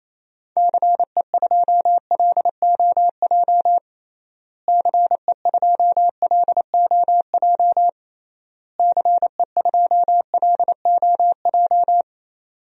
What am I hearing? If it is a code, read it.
Morse code, heard as a keyed tone, about 28 wpm, 700 Hz. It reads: CE2LOJ CE2LOJ CE2LOJ
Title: DXCC-100 Call Signs - 6 Characters - Repeated 3x - 28wpm